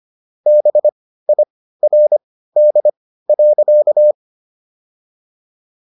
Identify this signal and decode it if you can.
Morse code, heard as a keyed tone, about 25 wpm, spaced as for 15 wpm, 600 Hz. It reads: BIRD.